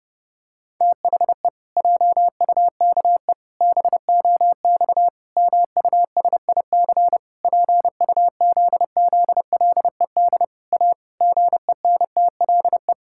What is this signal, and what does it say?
Morse code, heard as a keyed tone, about 30 wpm, 700 Hz. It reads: THE JUKE BOX MUSIC PUZZLED A GENTLE
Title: Pangram Challenge v2 - 30wpm